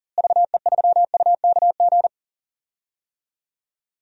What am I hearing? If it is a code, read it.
Morse code, heard as a keyed tone, about 37 wpm, 700 Hz. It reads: VE3UKG
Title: DXCC-100 Call Signs - 6 Characters - Repeated 3x - 40wpm